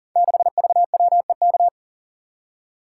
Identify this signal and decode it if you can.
Morse code, heard as a keyed tone, about 38 wpm, 700 Hz. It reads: 6VWEK
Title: Code Groups - 5 Letter:Numbers - 40wpm